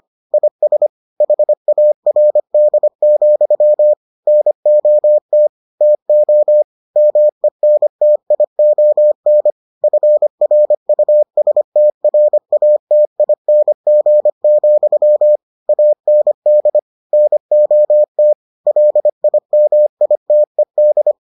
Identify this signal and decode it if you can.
Morse code, heard as a keyed tone, about 25 wpm, 600 Hz. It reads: IS HARD, NOT TO MENTION FRUSTRATING, AND NOT LIMITED